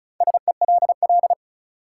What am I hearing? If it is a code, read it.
Morse code, heard as a keyed tone, about 36 wpm, 700 Hz. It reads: SELL
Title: Sentences from Top 1100 Words - New Word Drill Alt - 35wpm